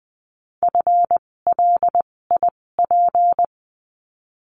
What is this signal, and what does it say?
Morse code, heard as a keyed tone, about 20 wpm, spaced as for 16 wpm, 700 Hz. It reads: FLIP
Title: 256 Lesson 42a - Words 12-20wpm